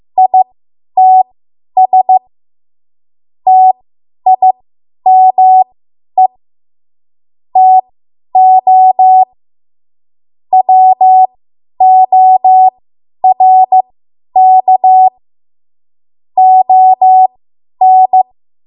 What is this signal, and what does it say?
Morse code, heard as a keyed tone, about 15 wpm, spaced as for 10 wpm, 750 Hz. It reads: ITS TIME TO WORK ON
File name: Morse / 10WPM